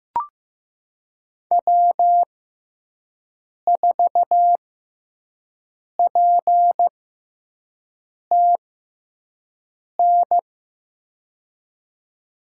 Morse code, heard as a keyed tone, about 15 wpm, spaced as for 5 wpm, 700 Hz. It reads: W4PTN
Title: USA Call Signs - 5 Characters - 6x Character Spacing - 15wpm